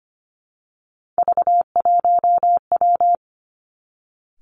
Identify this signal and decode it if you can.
Morse code, heard as a keyed tone, about 25 wpm, 700 Hz. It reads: V1W
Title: Three Challenging Letter-Number Combinations 25wpm